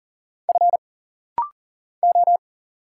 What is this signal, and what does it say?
Morse code, heard as a keyed tone, about 40 wpm, 700 Hz. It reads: F O
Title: Single Letters - Rapid-Fire 40wpm